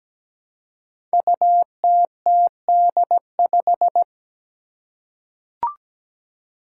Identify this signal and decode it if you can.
Morse code, heard as a keyed tone, about 17 wpm, 700 Hz. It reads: UTTD5
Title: Five Challenging Letter-Number Combinations 17wpm